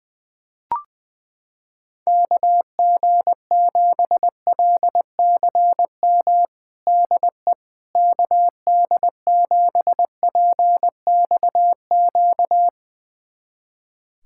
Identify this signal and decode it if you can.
Morse code, heard as a keyed tone, about 20 wpm, 700 Hz. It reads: KG7LCM DE KD7PXQ
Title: USA Callsign DE Callsign - 6 Characters - 20wpm